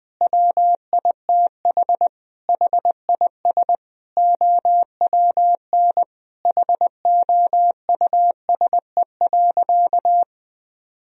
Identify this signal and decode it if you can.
Morse code, heard as a keyed tone, about 21 wpm, 700 Hz. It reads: WITH HIS OWN HOUSE.
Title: Sentences from Top 1000 Words - Review Part 1 - 20wpm